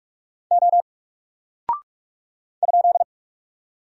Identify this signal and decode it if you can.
Morse code, heard as a keyed tone, about 44 wpm, 700 Hz. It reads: O ?
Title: Single Character (Letter:Number:Punctuation:Prosign) Alt 45wpm